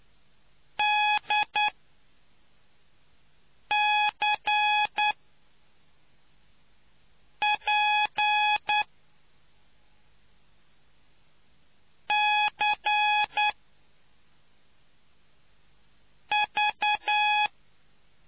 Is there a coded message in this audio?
Yes: Morse code from 0.8 to 17.5 s, heard as a keyed tone, about 9 wpm, spaced as for 3 wpm, 840 Hz. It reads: DCPCV